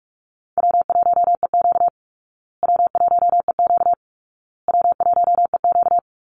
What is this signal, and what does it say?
Morse code, heard as a keyed tone, about 44 wpm, 700 Hz. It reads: W1EQ W1EQ W1EQ
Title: Easy Call Signs Repeated 3x - 45wpm